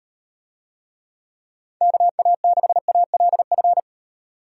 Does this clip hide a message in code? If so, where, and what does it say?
Yes: Morse code from 1.8 to 3.8 s, heard as a keyed tone, about 38 wpm, 700 Hz. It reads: KA6ALF